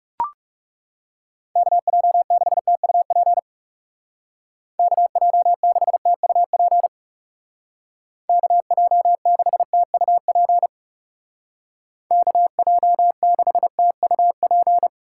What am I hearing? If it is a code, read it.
Morse code, heard as a keyed tone, about 36 wpm, 700 Hz. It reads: KJ6TUP KJ6TUP KJ6TUP KJ6TUP
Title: Speed Racing - USA 2x3 Call Signs 30wpm